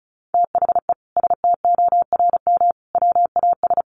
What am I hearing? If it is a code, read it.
Morse code, heard as a keyed tone, about 34 wpm, 700 Hz. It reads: THE STORM WAS
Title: Sentences from Top 1500 Words 35wpm